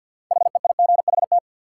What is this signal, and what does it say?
Morse code, heard as a keyed tone, about 47 wpm, 700 Hz. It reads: HEIGHT
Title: Sentences from Top 1600 Words - New Word Drill - 50wpm